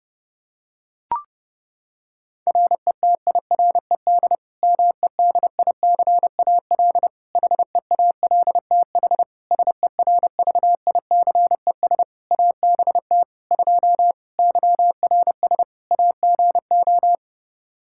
Morse code, heard as a keyed tone, about 30 wpm, 700 Hz. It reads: RETIRED MEDICAL HEALTH SERVICES ABT 2 YRS AGO
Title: Practice QSO - Exchange 3 - More Info - 30wpm